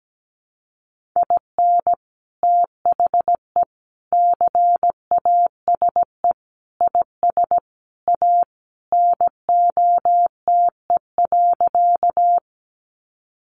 Morse code, heard as a keyed tone, about 17 wpm, 700 Hz. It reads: IN THE CASE IS A NOTE.
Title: Sentences from Top 700 Words 17wpm